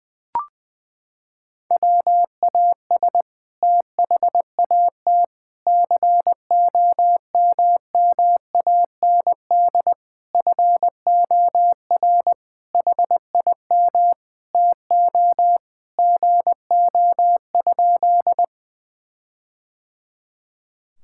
Morse code, heard as a keyed tone, about 19 wpm, 700 Hz. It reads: WAS THAT COMMAND FOR HIM TO GO?
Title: Sentences from Top 1600 Words 20wpm